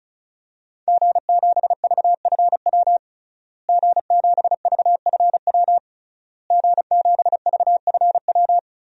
Morse code, heard as a keyed tone, about 36 wpm, 700 Hz. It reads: G7VFW G7VFW G7VFW
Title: DXCC-100 Call Signs - 5 Characters - Repeated 3x - 35wpm